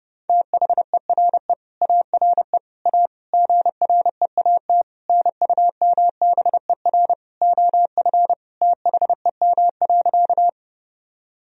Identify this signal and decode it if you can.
Morse code, heard as a keyed tone, about 30 wpm, 700 Hz. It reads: THERE ARE A GREAT NUMBER OF THEM.
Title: Sentences from Top 1000 Words - Review Part 1 - 30wpm